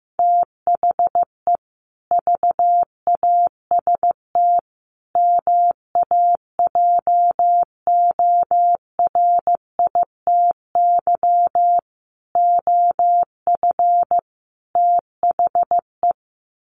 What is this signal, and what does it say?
Morse code, heard as a keyed tone, about 15 wpm, 700 Hz. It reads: THE VAST MAJORITY OF THE
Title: Sets of 5 Words Encore 15wpm